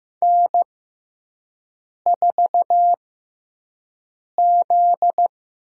Morse code, heard as a keyed tone, about 15 wpm, spaced as for 5 wpm, 700 Hz. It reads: N4Z
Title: USA Call Signs - 5 Characters - 6x Character Spacing - 15wpm